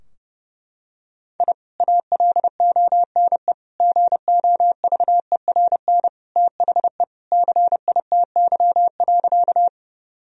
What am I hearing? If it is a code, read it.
Morse code, heard as a keyed tone, about 30 wpm, 700 Hz. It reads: I ALONE GOVERN THE CITY.
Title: Sentences from Top 1000 Words - Review Part 2 - 30wpm